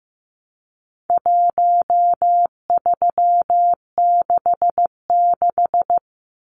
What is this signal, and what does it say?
Morse code, heard as a keyed tone, about 15 wpm, 700 Hz. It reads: 1366
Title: Numbers - Contest Style - 4-Digit 15wpm